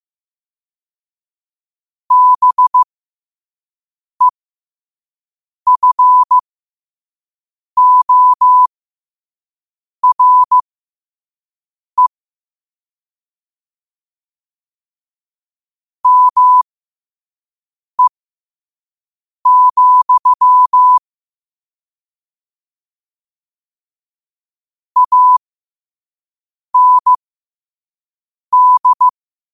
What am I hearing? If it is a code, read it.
Morse code, heard as a keyed tone, about 15 wpm, spaced as for 5 wpm, 1000 Hz. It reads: BEFORE ME, AND
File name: qotdfiles / 5wpm